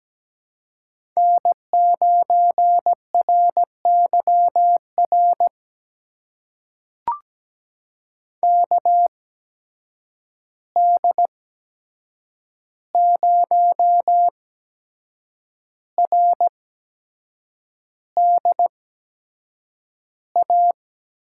Morse code, heard as a keyed tone, about 17 wpm, spaced as for 7 wpm, 700 Hz. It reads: N9RYR KD0RDA
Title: USA Call Signs - 8x Character Spacing - 17wpm